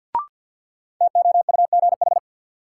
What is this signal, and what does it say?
Morse code, heard as a keyed tone, about 47 wpm, 700 Hz. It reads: TOUGH
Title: Sentences from Top 1200 Words - New Word Drill - 50wpm